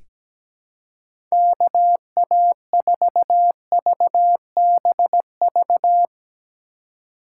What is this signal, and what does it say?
Morse code, heard as a keyed tone, about 17 wpm, 700 Hz. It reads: KA4VBV
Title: USA Call Signs - 2x Character Spacing - 17wpm